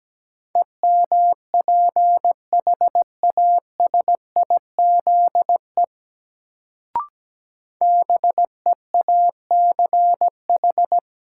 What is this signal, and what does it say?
Morse code, heard as a keyed tone, about 17 wpm, 700 Hz. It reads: EMPHASIZE BEACH
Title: Sentences from Top 1500 Words - New Word Drill - 17wpm